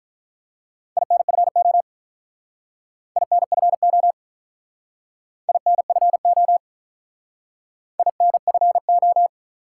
Morse code, heard as a keyed tone, about 48 wpm, 700 Hz. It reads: INFO INFO INFO INFO